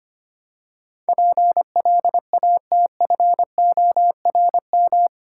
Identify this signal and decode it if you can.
Morse code, heard as a keyed tone, about 25 wpm, 700 Hz. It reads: PLATFORM
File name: 8 Letter Words - First 3 Letters Given - 25wpm